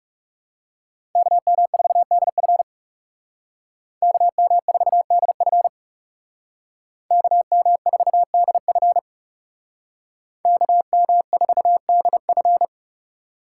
Morse code, heard as a keyed tone, about 39 wpm, 700 Hz. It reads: KM4DF KM4DF KM4DF KM4DF